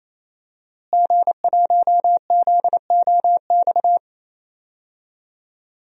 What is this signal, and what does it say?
Morse code, heard as a keyed tone, about 28 wpm, 700 Hz. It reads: G1ZOX
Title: DXCC-100 Call Signs - 5 Characters - Repeated 3x - 28wpm